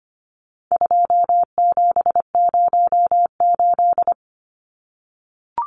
Morse code, heard as a keyed tone, about 25 wpm, 700 Hz. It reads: 2708
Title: Numbers - 4-Digit 25wpm